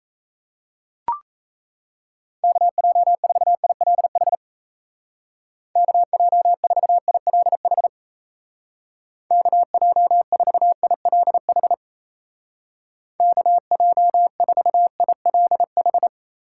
Morse code, heard as a keyed tone, about 34 wpm, 700 Hz. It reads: KJ4ILH KJ4ILH KJ4ILH KJ4ILH
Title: Speed Racing - USA 2x3 Call Signs 28wpm